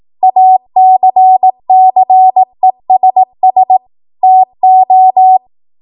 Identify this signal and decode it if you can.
Morse code, heard as a keyed tone, about 18 wpm, 750 Hz. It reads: ACCESS TO
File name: Morse / 18WPM